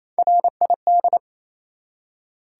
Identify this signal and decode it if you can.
Morse code, heard as a keyed tone, about 28 wpm, 700 Hz. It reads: RID
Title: Three Letter Words - 28wpm